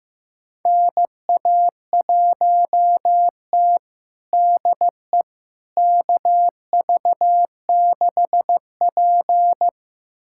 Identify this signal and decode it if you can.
Morse code, heard as a keyed tone, about 16 wpm, 700 Hz. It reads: NA1T DE KV6P
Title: USA Callsign DE Callsign - 4 Characters - 15wpm